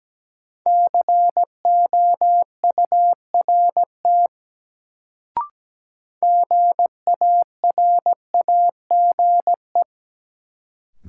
Morse code, heard as a keyed tone, about 17 wpm, 700 Hz. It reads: COURT GARAGE
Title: Sentences from Top 1600 Words - New Word Drill - 17wpm